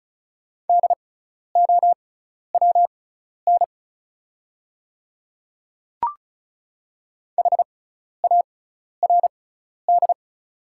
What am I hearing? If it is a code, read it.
Morse code, heard as a keyed tone, about 36 wpm, spaced as for 12 wpm, 690 Hz. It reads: DOWN HARD
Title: 6x Character Spacing - Top 300 Words - 35wpm